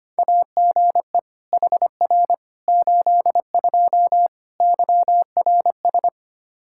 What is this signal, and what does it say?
Morse code, heard as a keyed tone, about 25 wpm, 700 Hz. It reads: AGE HR 82 YRS
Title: K1USN SST QSO Practice 25wpm